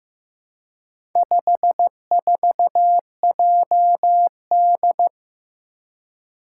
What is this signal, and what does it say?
Morse code, heard as a keyed tone, about 15 wpm, 700 Hz. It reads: 54JD
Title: Four Challenging Letter-Number Combinations 15wpm